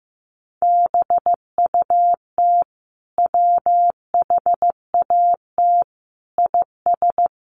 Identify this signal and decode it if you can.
Morse code, heard as a keyed tone, about 15 wpm, 700 Hz. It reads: BUT WHAT IS
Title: Sentences from Top 1300 Words 15wpm